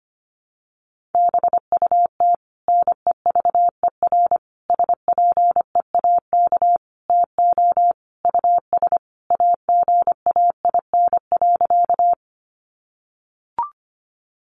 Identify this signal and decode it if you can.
Morse code, heard as a keyed tone, about 25 wpm, 700 Hz. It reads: BUT NEVER SPEAK TO US AGAIN.
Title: Sentences from Top 1100 Words 25wpm